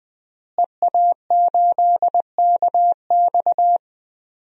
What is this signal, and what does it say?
Morse code, heard as a keyed tone, about 20 wpm, 700 Hz. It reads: EA8KX
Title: DXCC-100 Call Signs - 5 Characters - Repeated 3x - 20wpm